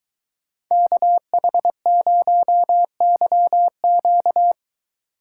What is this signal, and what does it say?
Morse code, heard as a keyed tone, about 22 wpm, 700 Hz. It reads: KH0YQ